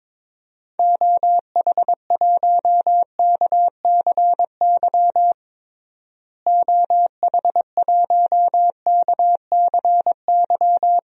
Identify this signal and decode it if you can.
Morse code, heard as a keyed tone, about 22 wpm, 700 Hz. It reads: OH1KCY OH1KCY
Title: DXCC-100 Call Signs - 6 Characters - Repeated 3x - 22wpm